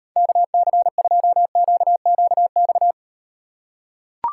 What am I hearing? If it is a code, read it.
Morse code, heard as a keyed tone, about 38 wpm, 700 Hz. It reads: KC2QQX